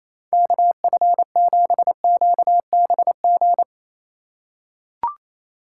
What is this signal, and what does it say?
Morse code, heard as a keyed tone, about 28 wpm, 700 Hz. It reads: KF7QBG
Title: USA Call Signs - 6 Characters - 2x Character Spacing - 28wpm